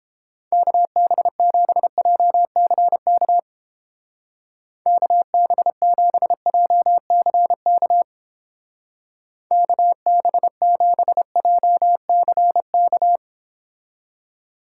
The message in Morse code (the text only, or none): KB7JCK KB7JCK KB7JCK